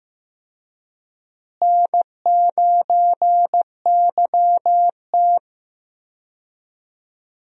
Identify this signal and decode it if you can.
Morse code, heard as a keyed tone, about 15 wpm, 700 Hz. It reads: N9YT
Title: USA Call Signs - 4 Characters - Repeated 2x - 15wpm